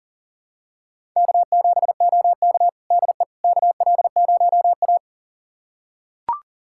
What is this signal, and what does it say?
Morse code, heard as a keyed tone, about 40 wpm, 700 Hz. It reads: K7OK DE KL0A
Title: USA Callsign DE Callsign - 4 Characters - 40wpm